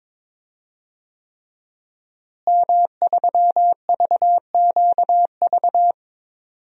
Morse code, heard as a keyed tone, about 21 wpm, 700 Hz. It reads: M3VQV
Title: DXCC-100 Call Signs - 5 Characters - Repeated 3x - 22wpm